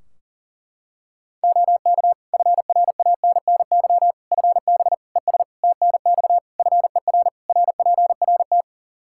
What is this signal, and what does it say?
Morse code, heard as a keyed tone, about 40 wpm, 700 Hz. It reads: OK FRANNY FB ES TNX FER RPRT
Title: Practice QSO - Exchange 2 - Info - 40wpm